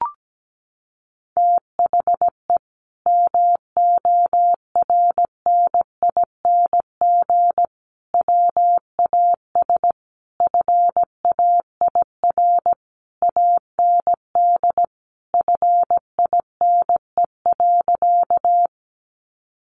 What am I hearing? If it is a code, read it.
Morse code, heard as a keyed tone, about 17 wpm, 700 Hz. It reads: THE MORNING WAS FAIR AND FINE.
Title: Sentences from Top 1300 Words 17wpm